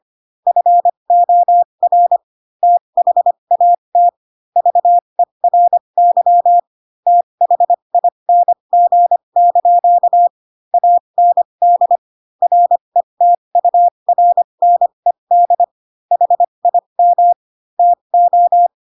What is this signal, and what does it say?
Morse code, heard as a keyed tone, about 25 wpm, spaced as for 22 wpm, 700 Hz. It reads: FOR THAT VERY THING) AND RETURNED HIM TO
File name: Pinocchio / Pinocchio ch3 18 25 ik1hge